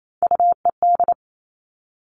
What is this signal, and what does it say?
Morse code, heard as a keyed tone, about 28 wpm, 700 Hz. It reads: UED
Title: All Cut Numbers - 3-Digit 28wpm